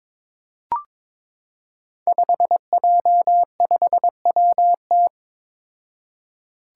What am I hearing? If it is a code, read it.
Morse code, heard as a keyed tone, about 22 wpm, 700 Hz. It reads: 5J5WT
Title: Five Challenging Letter-Number Combinations 22wpm